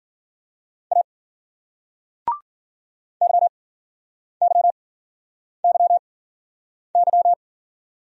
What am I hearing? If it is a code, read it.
Morse code, heard as a keyed tone, about 51 wpm, spaced as for 9 wpm, 700 Hz. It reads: A YYYY